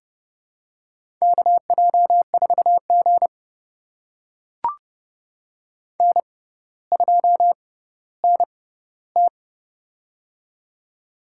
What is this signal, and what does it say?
Morse code, heard as a keyed tone, about 30 wpm, spaced as for 15 wpm, 700 Hz. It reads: KJ4G N2NT
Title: USA Call Signs - 4 Characters - 6x Character Spacing - 30wpm